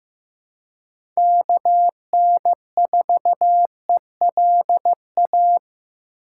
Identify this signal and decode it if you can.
Morse code, heard as a keyed tone, about 18 wpm, 700 Hz. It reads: KN4ELA
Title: Speed Racing - USA 2x3 Call Signs 15wpm